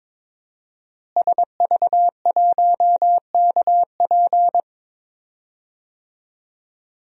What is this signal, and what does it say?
Morse code, heard as a keyed tone, about 22 wpm, 700 Hz. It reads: SV1KP
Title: DXCC-100 Call Signs - 5 Characters - Repeated 2x - 22wpm